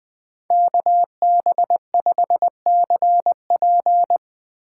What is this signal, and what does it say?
Morse code, heard as a keyed tone, about 22 wpm, 700 Hz. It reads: KB5CP
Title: Speed Racing - USA 2x2 Call Signs 20wpm